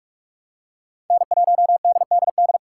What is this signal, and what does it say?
Morse code, heard as a keyed tone, about 38 wpm, 700 Hz. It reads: N1DDD